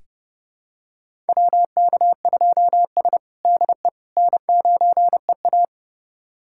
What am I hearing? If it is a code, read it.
Morse code, heard as a keyed tone, about 32 wpm, 700 Hz. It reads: WK2S DE N9EA
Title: USA Callsign DE Callsign - 4 Characters - 30wpm